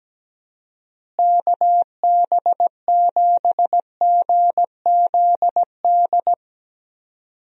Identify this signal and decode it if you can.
Morse code, heard as a keyed tone, about 17 wpm, 700 Hz. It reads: KB7GZD
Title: USA Call Signs - 6 Characters - Repeated 2x - 17wpm